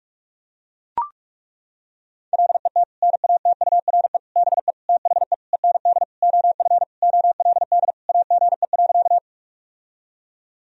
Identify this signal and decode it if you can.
Morse code, heard as a keyed tone, about 41 wpm, 700 Hz. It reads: LET NATURE BE THE END OF OLD AGE.
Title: Sentences from Top 1000 Words - Review Part 3 - 45wpm